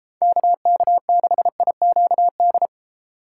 Morse code, heard as a keyed tone, about 33 wpm, 700 Hz. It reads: KK6IQD